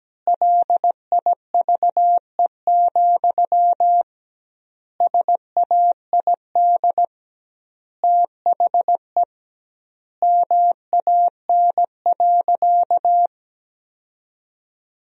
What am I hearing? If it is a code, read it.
Morse code, heard as a keyed tone, about 17 wpm, 700 Hz. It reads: LIVE, SAID THE MAN.
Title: 2x Word-Spacing - Sentences from Top 200 Words - 17wpm